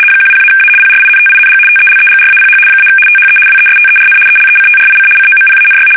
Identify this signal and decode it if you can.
RTTY, about 75 baud. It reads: 10-A PRINTING TELEGRAPH - THE WESTERN ELECTRIC COMPANY
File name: fsk850-75B / teletype-corp